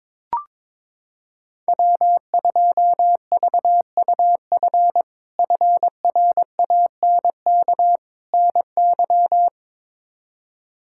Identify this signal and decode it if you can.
Morse code, heard as a keyed tone, about 22 wpm, 700 Hz. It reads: W2VUF FRANK NY
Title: K1USN SST QSO Practice - 22wpm